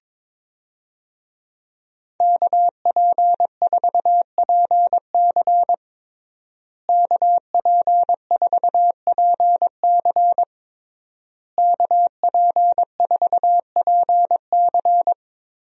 Morse code, heard as a keyed tone, about 22 wpm, 700 Hz. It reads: KP4PC KP4PC KP4PC
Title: DXCC-100 Call Signs - 5 Characters - Repeated 3x - 22wpm